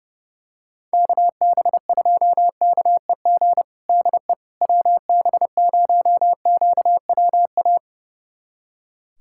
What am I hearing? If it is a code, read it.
Morse code, heard as a keyed tone, about 30 wpm, 700 Hz. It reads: KB2KEG DE WB0QWA
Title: USA Callsign DE Callsign 30wpm